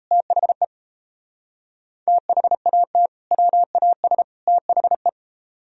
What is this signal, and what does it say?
Morse code, heard as a keyed tone, about 37 wpm, 700 Hz. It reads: THE THAT WAS THE